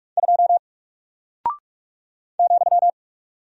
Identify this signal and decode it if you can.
Morse code, heard as a keyed tone, about 44 wpm, 700 Hz. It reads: 2 ,
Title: Single Character (Letter:Number:Punctuation:Prosign) Alt 45wpm